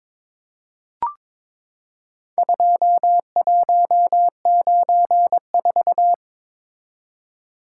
Morse code, heard as a keyed tone, about 22 wpm, 700 Hz. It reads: 2194
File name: practice / 22wpm / Four Letter-Number Combinations - 22wpm